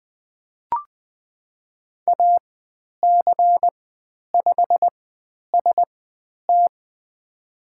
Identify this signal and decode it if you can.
Morse code, heard as a keyed tone, about 20 wpm, spaced as for 10 wpm, 700 Hz. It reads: AC5ST
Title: 102 Lesson 17g - Letter u - Call Signs 20wpm